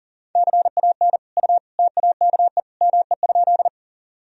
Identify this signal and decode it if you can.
Morse code, heard as a keyed tone, about 37 wpm, 700 Hz. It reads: CAN U TAKE ME?
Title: Sentences from Top 200 Words and Abbreviations 40wpm